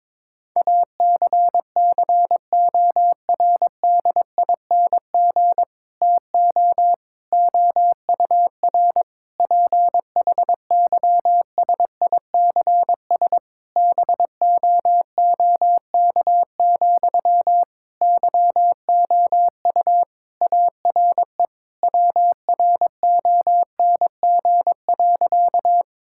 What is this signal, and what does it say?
Morse code, heard as a keyed tone, about 22 wpm, 700 Hz. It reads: ACCORDING TO OUR PHYSICS BOOK, YOU ARE WRONG.
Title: Sentences from Top 1000 Words - Review Part 5 - 22wpm